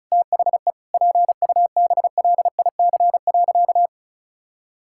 Morse code, heard as a keyed tone, about 36 wpm, 700 Hz. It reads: THE PUBLIC.
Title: Sentences from Top 1000 Words - Review Part 6 - 35wpm